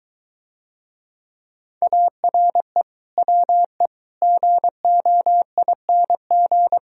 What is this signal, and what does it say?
Morse code, heard as a keyed tone, about 23 wpm, 690 Hz. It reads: ARE WE GOING